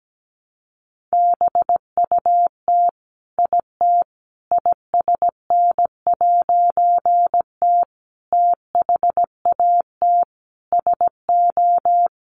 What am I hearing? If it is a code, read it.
Morse code, heard as a keyed tone, about 17 wpm, 700 Hz. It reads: BUT IT ISN'T THAT SO
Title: A Sound Of Thunder 17wpm